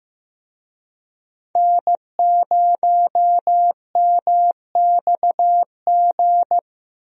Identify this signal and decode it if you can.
Morse code, heard as a keyed tone, about 15 wpm, 700 Hz. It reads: N0MXG
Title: USA Call Signs - 5 Characters - 15wpm